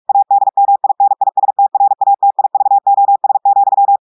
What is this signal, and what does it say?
Morse code, heard as a keyed tone, about 46 wpm, 800 Hz. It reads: ADMINISTRATIVOS,
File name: audios / portugais / portugais texte en clair 2 vitesse 45 mots minutes